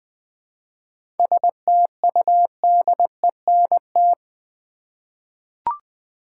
Morse code, heard as a keyed tone, about 20 wpm, 700 Hz. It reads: STUDENT
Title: Seven Letter Words - 20wpm